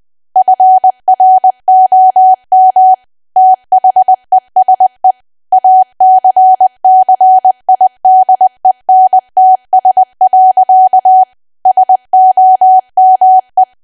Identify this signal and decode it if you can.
Morse code, heard as a keyed tone, about 20 wpm, 750 Hz. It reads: FROM THESE ACCIDENTS. SOME
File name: cwa / W1AW-Level3 / 20WPM